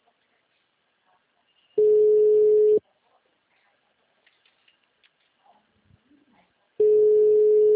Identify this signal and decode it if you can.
Ringback tone (Europe, 425 Hz).